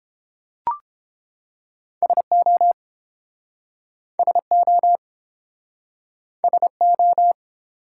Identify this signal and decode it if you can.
Morse code, heard as a keyed tone, about 33 wpm, 700 Hz. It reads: SO SO SO